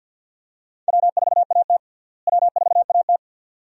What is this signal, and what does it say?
Morse code, heard as a keyed tone, about 53 wpm, 700 Hz. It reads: W4AT W4AT
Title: USA Call Signs - 4 Characters - Repeated 2x - 50wpm